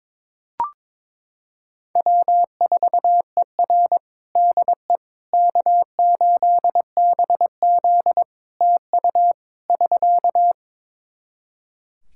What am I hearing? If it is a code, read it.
Morse code, heard as a keyed tone, about 22 wpm, 700 Hz. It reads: W4ER DE K8BZ TU <SK>
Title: Practice QSO - Exchange 4 - Ending - 22wpm